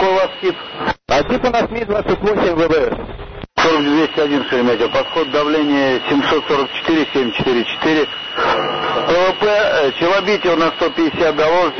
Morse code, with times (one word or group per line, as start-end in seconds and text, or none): none